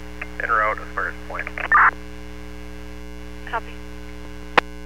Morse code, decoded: none

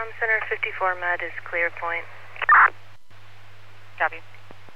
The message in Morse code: none

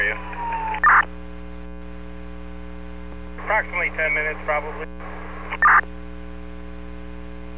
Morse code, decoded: none

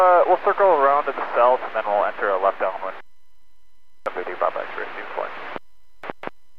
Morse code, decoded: none

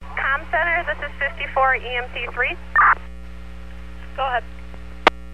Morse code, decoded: none

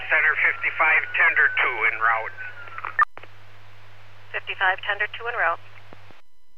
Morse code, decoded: none

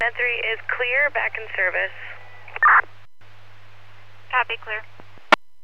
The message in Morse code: none

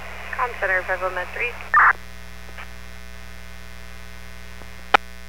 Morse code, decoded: none

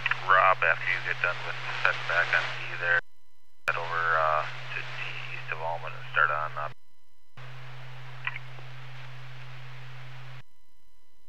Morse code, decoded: none

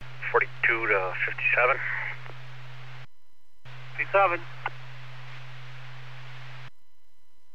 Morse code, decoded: none

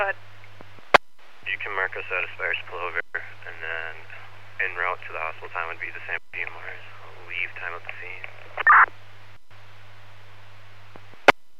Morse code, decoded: none

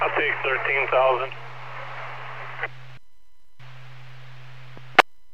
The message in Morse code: none

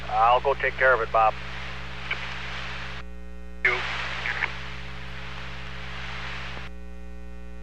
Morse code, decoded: none

